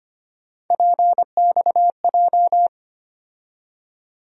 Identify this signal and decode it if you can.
Morse code, heard as a keyed tone, about 25 wpm, 700 Hz. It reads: PXJ
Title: USA Call Sign Postfixes - 3 Letters 25wpm